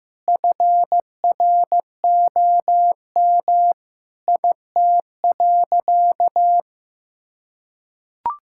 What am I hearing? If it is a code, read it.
Morse code, heard as a keyed tone, about 15 wpm, 700 Hz. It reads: FROM IT.
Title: Sentences from Top 1000 Words - Review Part 6 - 15wpm